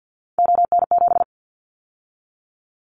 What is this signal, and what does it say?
Morse code, heard as a keyed tone, about 50 wpm, 700 Hz. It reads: ON7
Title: Callsign Prefixes for Top 50 DXCC - 50wpm